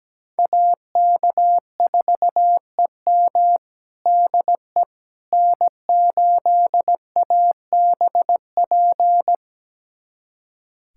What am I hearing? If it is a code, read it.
Morse code, heard as a keyed tone, about 17 wpm, 700 Hz. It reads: AK4EM DE N8ABP
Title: USA Callsign DE Callsign - 5 Characters - 17wpm